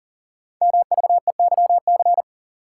Morse code, heard as a keyed tone, about 42 wpm, 700 Hz. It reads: MVEYC
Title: Code Groups - 5 Letters - 40wpm